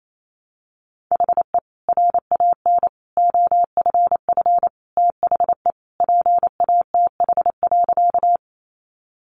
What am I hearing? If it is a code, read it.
Morse code, heard as a keyed tone, about 28 wpm, 700 Hz. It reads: HE RAN OFF THE PATH.
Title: A Sound Of Thunder 28wpm